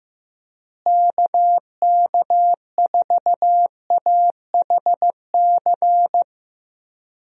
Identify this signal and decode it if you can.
Morse code, heard as a keyed tone, about 15 wpm, 700 Hz. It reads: KK4AHC
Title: USA Call Signs - 6 Characters - Repeated 3x - 15wpm